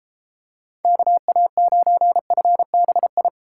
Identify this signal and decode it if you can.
Morse code, heard as a keyed tone, about 33 wpm, 700 Hz. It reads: KA9FBI